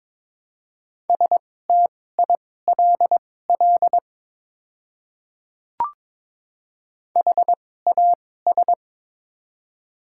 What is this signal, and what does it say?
Morse code, heard as a keyed tone, about 22 wpm, spaced as for 16 wpm, 700 Hz. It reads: STILL HAS
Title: 2x Character Spacing - Top 300 Words - 22wpm